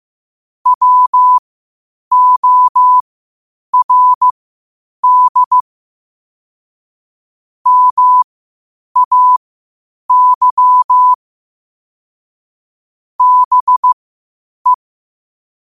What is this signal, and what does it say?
Morse code, heard as a keyed tone, about 15 wpm, spaced as for 8 wpm, 1000 Hz. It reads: WORD MAY BE
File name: qotdfiles / 8wpm